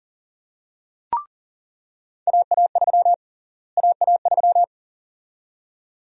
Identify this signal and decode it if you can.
Morse code, heard as a keyed tone, about 40 wpm, 700 Hz. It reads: AA3 AA3
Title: USA Call Sign Prefixes - Repeated 2x - 40wpm